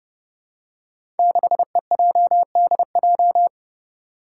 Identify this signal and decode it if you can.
Morse code, heard as a keyed tone, about 29 wpm, 700 Hz. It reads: 6EJDJ
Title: Five Challenging Letter-Number Combinations 30wpm